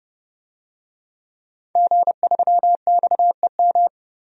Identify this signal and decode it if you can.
Morse code, heard as a keyed tone, about 30 wpm, 700 Hz. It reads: G3XEM
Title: DXCC-100 Call Signs - 5 Characters - 30wpm